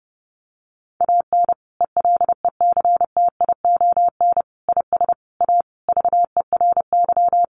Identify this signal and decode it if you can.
Morse code, heard as a keyed tone, about 30 wpm, 700 Hz. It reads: AN ELECTION IS A VERY
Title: Sentences from Top 1500 Words 30wpm